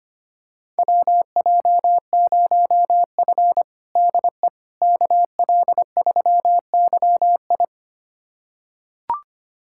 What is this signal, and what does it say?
Morse code, heard as a keyed tone, about 25 wpm, 700 Hz. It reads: WJ0F DE KL3YI
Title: USA Callsign DE Callsign 25wpm